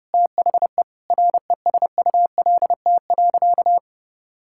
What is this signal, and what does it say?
Morse code, heard as a keyed tone, about 30 wpm, 700 Hz. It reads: THE RESULT.
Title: Sentences from Top 1000 Words - Review Part 5 - 30wpm